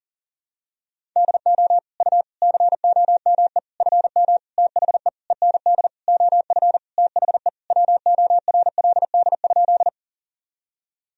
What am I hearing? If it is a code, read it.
Morse code, heard as a keyed tone, about 40 wpm, 700 Hz. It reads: DO U COME FM THE END OF THE WORLD?
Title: Sentences from Top 200 Words and Abbreviations 40wpm